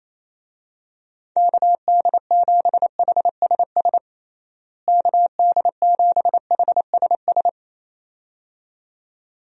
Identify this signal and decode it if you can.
Morse code, heard as a keyed tone, about 28 wpm, 700 Hz. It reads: KD7HSS KD7HSS
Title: USA Call Signs - 6 Characters - Repeated 2x - 28wpm